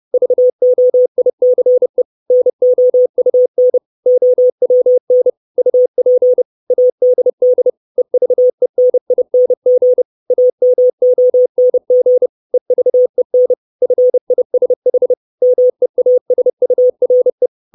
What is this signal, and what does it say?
Morse code, heard as a keyed tone, about 30 wpm, 500 Hz. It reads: VOICE NOUN OWN UP ADD EVENING AMONG EVEN FISH MEASURE
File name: 632 Words 30WPM